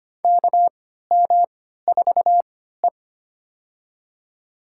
Morse code, heard as a keyed tone, about 25 wpm, spaced as for 14 wpm, 700 Hz. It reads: KM4E
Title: USA Call Signs - 4 Characters - 3x Character Spacing - 25wpm